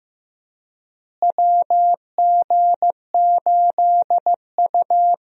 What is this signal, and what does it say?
Morse code, heard as a keyed tone, about 15 wpm, 700 Hz. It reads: WG8U
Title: USA Call Signs - 4 Characters - Repeated 2x - 15wpm